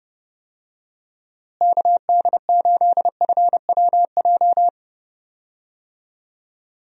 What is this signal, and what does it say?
Morse code, heard as a keyed tone, about 32 wpm, 700 Hz. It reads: KD8FWJ
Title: USA Call Signs - 6 Characters - Repeated 3x - 30wpm